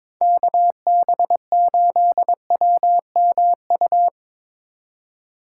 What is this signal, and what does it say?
Morse code, heard as a keyed tone, about 25 wpm, 700 Hz. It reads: KB8WMU
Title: Speed Racing - USA 2x3 Call Signs 22wpm